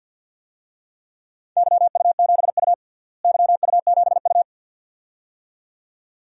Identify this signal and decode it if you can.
Morse code, heard as a keyed tone, about 50 wpm, 700 Hz. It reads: YU7U YU7U
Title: DXCC-100 Call Signs - 4 Characters - Repeated 2x - 50wpm